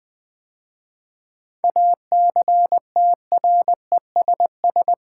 Morse code, heard as a keyed tone, about 20 wpm, 700 Hz. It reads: ACTRESS
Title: 7 Letter Words - First 3 Letters Given - 20wpm